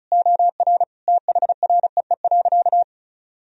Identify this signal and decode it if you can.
Morse code, heard as a keyed tone, about 36 wpm, 700 Hz. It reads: OR THREE.
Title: Sentences from Top 1000 Words - Review Part 1 - 35wpm